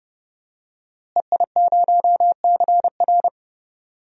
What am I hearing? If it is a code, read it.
Morse code, heard as a keyed tone, about 30 wpm, 700 Hz. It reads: EI0CR
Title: DXCC-100 Call Signs - 5 Characters - Repeated 2x - 30wpm